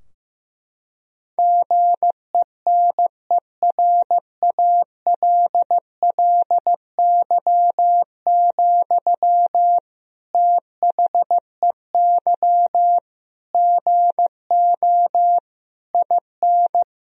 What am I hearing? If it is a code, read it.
Morse code, heard as a keyed tone, about 15 wpm, 700 Hz. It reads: GENERALLY, THEY GO IN
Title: Sentences from Top 1000 Words - Review Part 4 - 15wpm